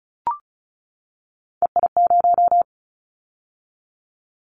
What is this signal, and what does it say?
Morse code, heard as a keyed tone, about 36 wpm, 700 Hz. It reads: EI0
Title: Callsign Prefixes for Top 50 DXCC - 35wpm